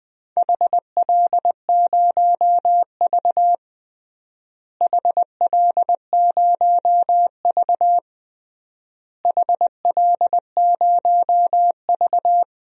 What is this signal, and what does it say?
Morse code, heard as a keyed tone, about 20 wpm, 700 Hz. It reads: HL0V HL0V HL0V
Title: DXCC-100 Call Signs - 4 Characters - Repeated 3x - 20wpm